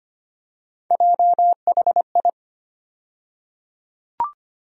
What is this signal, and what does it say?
Morse code, heard as a keyed tone, about 25 wpm, 700 Hz. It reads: JHI
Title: USA Call Sign Postfixes - 3 Letters 25wpm